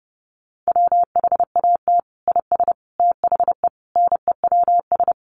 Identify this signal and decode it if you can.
Morse code, heard as a keyed tone, about 30 wpm, 700 Hz. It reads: WHAT IS THE NEWS
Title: Sentences from Top 800 Words 30wpm